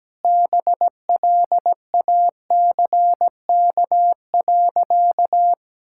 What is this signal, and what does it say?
Morse code, heard as a keyed tone, about 17 wpm, 700 Hz. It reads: BLACK.
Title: Sentences from Top 1600 Words 17wpm